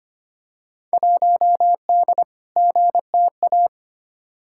Morse code, heard as a keyed tone, about 25 wpm, 700 Hz. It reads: 1D GTA
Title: Field Day Exchanges - 25wpm